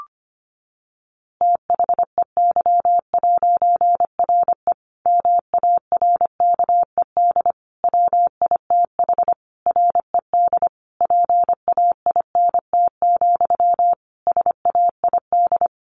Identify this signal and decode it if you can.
Morse code, heard as a keyed tone, about 25 wpm, 700 Hz. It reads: THEY'RE MARKED WITH RED PAINT, SAID
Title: A Sound Of Thunder 25wpm